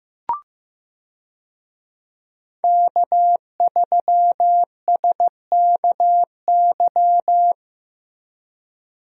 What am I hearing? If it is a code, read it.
Morse code, heard as a keyed tone, about 15 wpm, 700 Hz. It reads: K3SKY
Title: USA Call Signs - 5 Characters - Repeated 2x - 15wpm